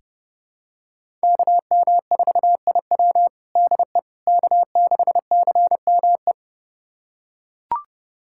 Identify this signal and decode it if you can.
Morse code, heard as a keyed tone, about 32 wpm, 700 Hz. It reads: KM4IW DE K6CME
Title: USA Callsign DE Callsign - 5 Characters - 30wpm